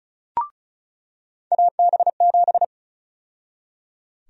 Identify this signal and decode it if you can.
Morse code, heard as a keyed tone, about 34 wpm, 700 Hz. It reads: AB7
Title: USA Call Sign Prefixes - 35wpm